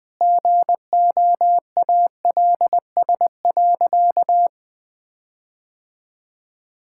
Morse code, heard as a keyed tone, about 20 wpm, 700 Hz. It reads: GOALS.
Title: Difficult Sentences with Hints - 20wpm